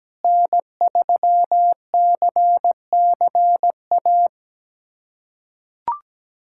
Morse code, heard as a keyed tone, about 17 wpm, 700 Hz. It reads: N3CCA
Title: USA Callsign DE Callsign - 5 Characters - 17wpm